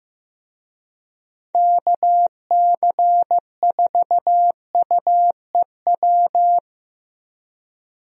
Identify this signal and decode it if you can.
Morse code, heard as a keyed tone, about 15 wpm, 700 Hz. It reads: KC4UEW
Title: USA Call Signs - 6 Characters - Repeated 2x - 15wpm